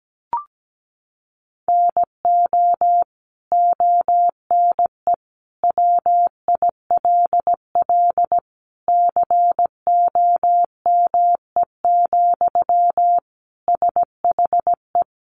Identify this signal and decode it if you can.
Morse code, heard as a keyed tone, about 17 wpm, 700 Hz. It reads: NO ONE WILL COME, SHE
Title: Sentences from Top 100 Words 17wpm